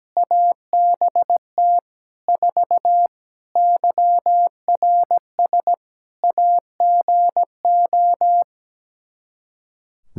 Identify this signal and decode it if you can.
Morse code, heard as a keyed tone, about 17 wpm, 700 Hz. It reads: ABT 4 YRS AGO
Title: Practice QSO - Exchange 3 - More Info - 17wpm